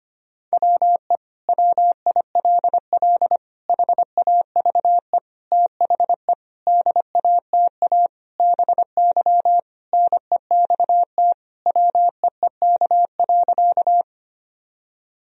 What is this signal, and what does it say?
Morse code, heard as a keyed tone, about 25 wpm, 700 Hz. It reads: WE WILL HAVE THE DATA BY NEXT WEEK.
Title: Sentences from Top 1000 Words - Review Part 3 - 25wpm